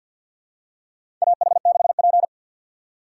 Morse code, heard as a keyed tone, about 50 wpm, 700 Hz. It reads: AH6P
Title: USA Call Signs - 4 Characters - 3x Character Spacing - 50wpm